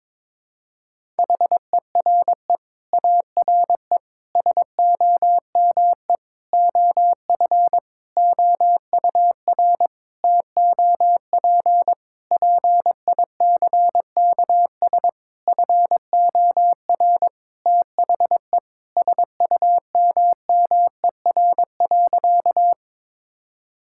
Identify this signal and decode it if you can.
Morse code, heard as a keyed tone, about 21 wpm, 700 Hz. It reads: HERE ARE SOME OF OUR TOP PICKS FOR THE SUMMER.
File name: Sentences with 9-Word Phrases - 22wpm